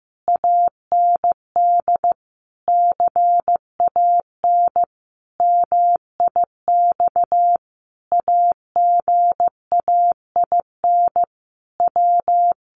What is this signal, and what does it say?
Morse code, heard as a keyed tone, about 15 wpm, 700 Hz. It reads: AND CAN MIX AGAIN W
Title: Sentences from Top 1100 Words 15wpm